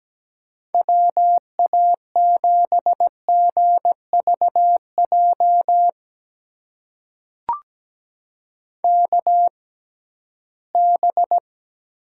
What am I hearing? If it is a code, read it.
Morse code, heard as a keyed tone, about 17 wpm, spaced as for 13 wpm, 700 Hz. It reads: WA7GVJ KB
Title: USA Call Signs - 6 Characters - 6x Character Spacing - 17wpm